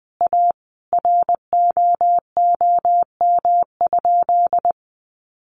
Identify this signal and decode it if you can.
Morse code, heard as a keyed tone, about 20 wpm, 700 Hz. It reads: A ROOM?
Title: Sentences from Top 1300 Words 20wpm